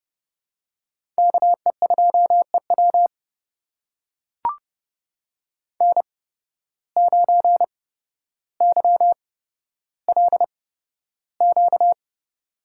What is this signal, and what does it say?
Morse code, heard as a keyed tone, about 32 wpm, spaced as for 13 wpm, 700 Hz. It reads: KE2EW N9YLQ
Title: USA Call Signs - 5 Characters - 8x Character Spacing - 30wpm